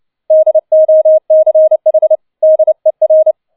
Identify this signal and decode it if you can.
Morse code, heard as a keyed tone, about 29 wpm, 630 Hz. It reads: DOCH DER
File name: 007 Letzter Streich 30wpm